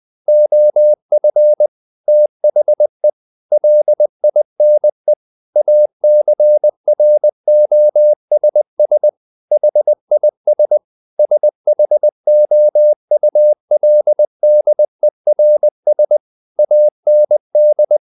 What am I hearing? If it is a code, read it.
Morse code, heard as a keyed tone, about 20 wpm, 600 Hz. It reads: OF THE LINE ACROSS HIS SHOULDERS AND
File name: the old man and the sea part 2 20wpm